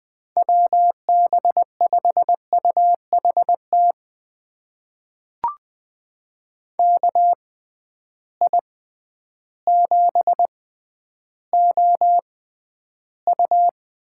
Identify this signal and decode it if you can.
Morse code, heard as a keyed tone, about 20 wpm, spaced as for 11 wpm, 700 Hz. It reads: WB5UHT KI7OU
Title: USA Call Signs - 6x Character Spacing - 20wpm